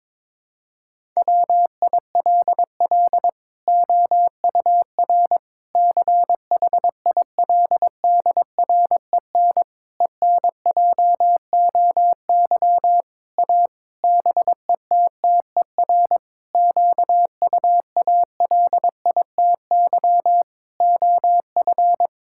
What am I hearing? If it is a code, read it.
Morse code, heard as a keyed tone, about 22 wpm, 700 Hz. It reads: WILL OUR CHILDREN ENJOY A BETTER QUALITY OF
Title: Sentences from Top 1000 Words - Review Part 5 - 22wpm